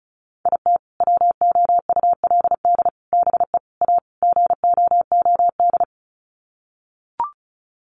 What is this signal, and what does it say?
Morse code, heard as a keyed tone, about 34 wpm, 700 Hz. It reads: IT WOULD BE A GOOD
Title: Sets of 5 Words Encore2 35wpm